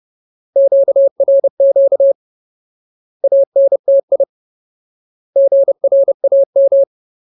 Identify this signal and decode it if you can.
Morse code, heard as a keyed tone, about 30 wpm, 550 Hz. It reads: QRQ ANTI GRAM
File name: Ultimate Word List / 30WPM Ultimate